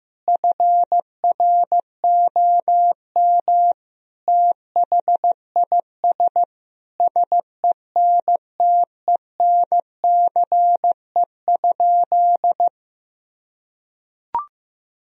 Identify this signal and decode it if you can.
Morse code, heard as a keyed tone, about 15 wpm, 700 Hz. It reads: FROM THIS SENTENCE?
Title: Sentences from Top 1000 Words - Review Part 1 - 15wpm